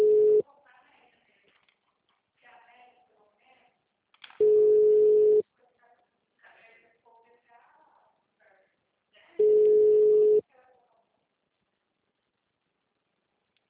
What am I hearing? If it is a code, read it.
Ringback tone (Europe, 425 Hz), with speech.